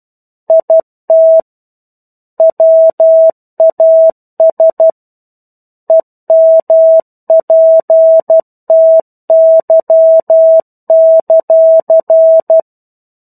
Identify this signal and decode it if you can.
Morse code, heard as a keyed tone, about 12 wpm, 650 Hz. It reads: IT WAS EMPTY;
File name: 01-Chapter-01 12 WPM